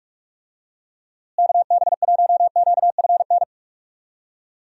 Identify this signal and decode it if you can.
Morse code, heard as a keyed tone, about 46 wpm, 700 Hz. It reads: KB1QFN
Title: USA Call Signs - 6 Characters - Repeated 2x - 45wpm